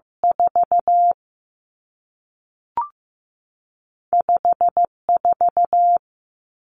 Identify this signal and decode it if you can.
Morse code, heard as a keyed tone, about 15 wpm, 700 Hz. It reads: 4 5 4
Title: Two Challenging Letter-Number Combinations 15wpm